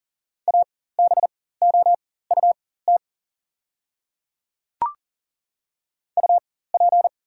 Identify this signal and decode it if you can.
Morse code, heard as a keyed tone, about 37 wpm, spaced as for 18 wpm, 700 Hz. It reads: ABOUT UP
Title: 4x Character Spacing - Top 100 Words - 40wpm